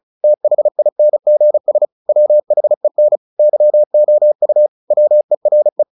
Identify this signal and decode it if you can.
Morse code, heard as a keyed tone, about 34 wpm, 600 Hz. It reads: THINGS WHEN YOU WERE
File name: the old man and the sea / the old man and the sea 1 35wpm